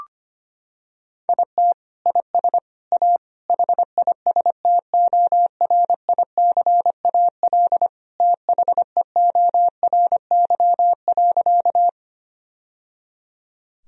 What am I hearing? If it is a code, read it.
Morse code, heard as a keyed tone, about 25 wpm, 700 Hz. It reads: IT IS A HISTORICAL THEORY.
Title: Sentences from Top 1000 Words - Review Part 4 - 25wpm